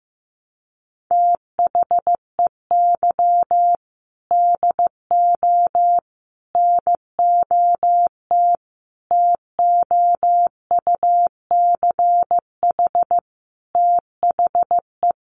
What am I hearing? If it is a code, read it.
Morse code, heard as a keyed tone, about 15 wpm, 700 Hz. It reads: THEY DO NOT TOUCH THE
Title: Sentences from Top 1200 Words 15wpm